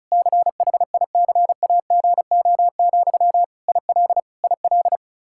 Morse code, heard as a keyed tone, about 36 wpm, 700 Hz. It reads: CHICAGO, IL IL
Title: K1USN SST QSO Practice 35wpm